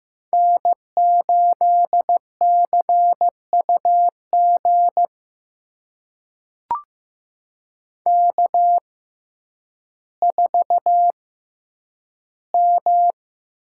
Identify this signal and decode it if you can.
Morse code, heard as a keyed tone, about 15 wpm, spaced as for 9 wpm, 700 Hz. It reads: N8CUG K4M
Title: USA Call Signs - 5 Characters - 6x Character Spacing - 15wpm